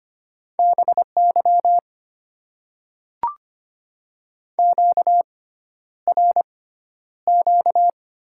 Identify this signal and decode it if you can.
Morse code, heard as a keyed tone, about 25 wpm, spaced as for 11 wpm, 690 Hz. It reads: BY QRQ
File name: Top 90 QSO Elements - 6x Character Spacing - 25wpm